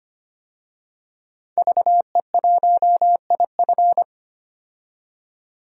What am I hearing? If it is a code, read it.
Morse code, heard as a keyed tone, about 25 wpm, 700 Hz. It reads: VE1IF
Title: DXCC-100 Call Signs - 5 Characters - 25wpm